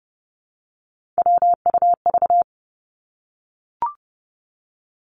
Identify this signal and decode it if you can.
Morse code, heard as a keyed tone, about 30 wpm, 700 Hz. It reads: WUV
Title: Three Challenging Letter-Number Combinations 30wpm